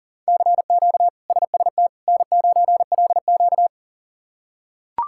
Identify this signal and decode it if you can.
Morse code, heard as a keyed tone, about 40 wpm, 700 Hz. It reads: CQ SST N9LQ
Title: K1USN SST QSO Practice - 40wpm